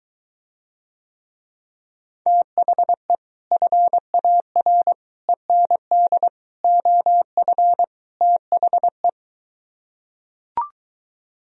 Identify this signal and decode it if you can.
Morse code, heard as a keyed tone, about 22 wpm, 700 Hz. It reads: THE FAR END OF THE